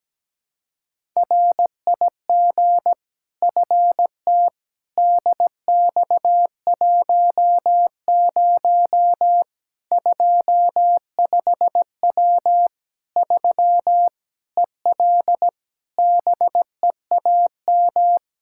Morse code, heard as a keyed tone, about 17 wpm, 700 Hz. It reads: RIG FT DX10 25W 3 EL BEAM
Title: Practice QSO - Exchange 2 - Info - 17wpm